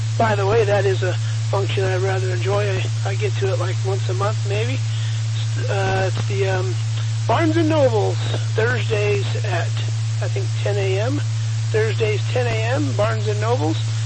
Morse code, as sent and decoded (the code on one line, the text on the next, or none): none